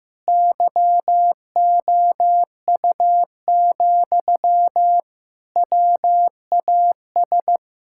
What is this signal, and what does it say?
Morse code, heard as a keyed tone, about 16 wpm, 700 Hz. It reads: YOU, WAS
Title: Sentences from Top 1000 Words - Review Part 6 - 15wpm